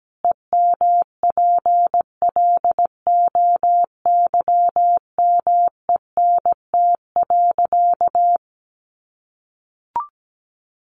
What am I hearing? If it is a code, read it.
Morse code, heard as a keyed tone, about 17 wpm, 700 Hz. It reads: EMPLOYMENT.
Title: Sentences from Top 1500 Words 17wpm